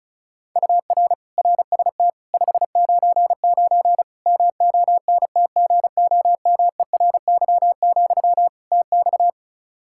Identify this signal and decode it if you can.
Morse code, heard as a keyed tone, about 36 wpm, 700 Hz. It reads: UR RST 599 MONTGOMERY, TX
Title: K1USN SST QSO Practice 35wpm